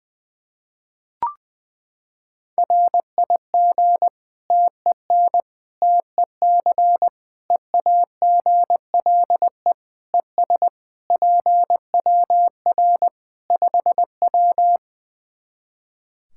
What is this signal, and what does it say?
Morse code, heard as a keyed tone, about 20 wpm, 700 Hz. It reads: RIG TEN TEC EAGLE ES PWR 5W
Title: Practice QSO - Exchange 2 - Info - 20wpm